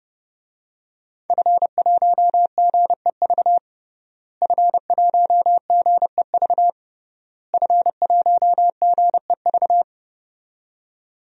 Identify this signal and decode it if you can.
Morse code, heard as a keyed tone, about 32 wpm, 700 Hz. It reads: F1GEV F1GEV F1GEV
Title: DXCC-100 Call Signs - 5 Characters - Repeated 3x - 30wpm